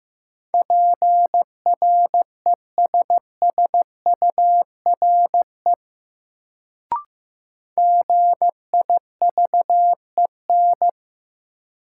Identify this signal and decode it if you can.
Morse code, heard as a keyed tone, about 15 wpm, 700 Hz. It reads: PRESSURE GIVEN
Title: Sentences from Top 1400 Words - New Word Drill Alt - 15wpm